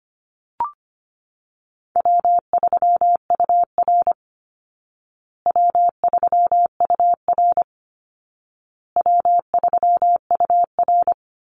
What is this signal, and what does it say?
Morse code, heard as a keyed tone, about 25 wpm, 700 Hz. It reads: W3UR W3UR W3UR
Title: Easy Call Signs Repeated 3x - 25wpm